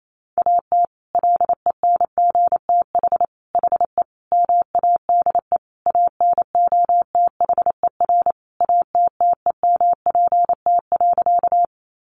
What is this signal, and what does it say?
Morse code, heard as a keyed tone, about 28 wpm, 700 Hz. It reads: AT LENGTH HE MADE ANOTHER ATTEMPT.
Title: Sentences from Top 1000 Words 28wpm